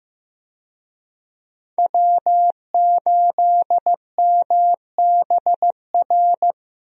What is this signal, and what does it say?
Morse code, heard as a keyed tone, about 15 wpm, 700 Hz. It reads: W8MBR
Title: USA Call Signs - 5 Characters - Repeated 2x - 15wpm